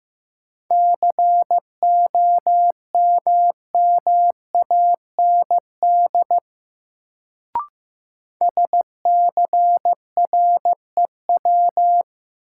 Morse code, heard as a keyed tone, about 15 wpm, 700 Hz. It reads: COMMAND SCREW
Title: Sentences from Top 1600 Words - New Word Drill Alt - 15wpm